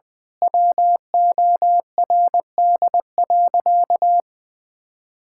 Morse code, heard as a keyed tone, about 20 wpm, 700 Hz. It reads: WORD.
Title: Pangram Challenge v2 - 20wpm